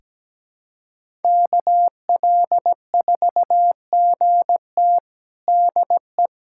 Morse code, heard as a keyed tone, about 17 wpm, 700 Hz. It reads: KL4GT DE
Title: USA Callsign DE Callsign 17wpm